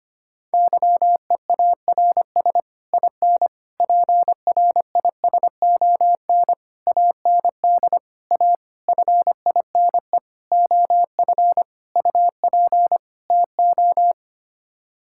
Morse code, heard as a keyed tone, about 25 wpm, 700 Hz. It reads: YEARS IN PRISON AND A FINE OF UP TO
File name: Sets of 9 Words - 25wpm